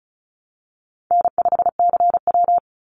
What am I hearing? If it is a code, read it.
Morse code, heard as a keyed tone, about 34 wpm, 700 Hz. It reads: N5CW
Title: Easy Call Signs Repeated 2x - 35wpm